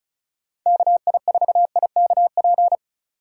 Morse code, heard as a keyed tone, about 36 wpm, 700 Hz. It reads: KI4IKP
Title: Speed Racing - USA 2x3 Call Signs 30wpm